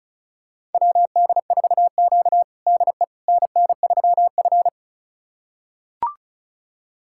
Morse code, heard as a keyed tone, about 36 wpm, 700 Hz. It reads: WD4Q DE NN3F
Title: USA Callsign DE Callsign - 4 Characters - 35wpm